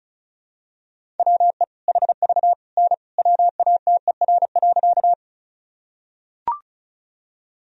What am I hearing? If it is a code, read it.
Morse code, heard as a keyed tone, about 34 wpm, 700 Hz. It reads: WE HV N WATER.
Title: Sentences from Top 100 Words and Abbreviations 35wpm